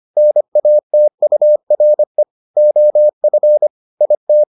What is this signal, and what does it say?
Morse code, heard as a keyed tone, about 25 wpm, 600 Hz. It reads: NATURE OF IT